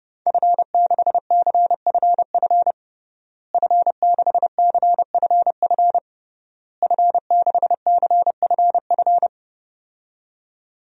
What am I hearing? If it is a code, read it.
Morse code, heard as a keyed tone, about 30 wpm, 700 Hz. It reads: F6CFF F6CFF F6CFF
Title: DXCC-100 Call Signs - 5 Characters - Repeated 3x - 30wpm